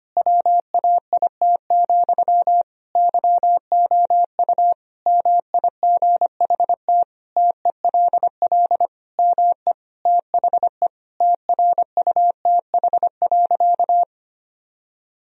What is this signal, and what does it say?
Morse code, heard as a keyed tone, about 25 wpm, 700 Hz. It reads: WAIT, YOU MIGHT TELL ME THE TRUTH.
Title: Sentences from Top 1000 Words - Review Part 1 - 25wpm